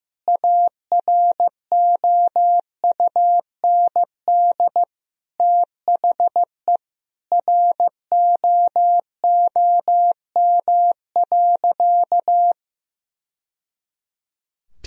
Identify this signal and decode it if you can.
Morse code, heard as a keyed tone, about 15 wpm, 700 Hz. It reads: AROUND THE ROOM.
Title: Sentences from Top 1000 Words - Review Part 3 - 15wpm